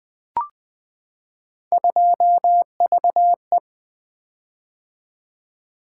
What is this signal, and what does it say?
Morse code, heard as a keyed tone, about 20 wpm, 700 Hz. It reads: 2VE
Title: Three Letter-Number Combinations - 20wpm